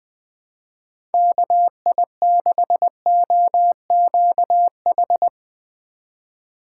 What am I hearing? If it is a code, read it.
Morse code, heard as a keyed tone, about 21 wpm, 700 Hz. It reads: KI6OQH
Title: USA Call Signs - 6 Characters - 3x Character Spacing - 20wpm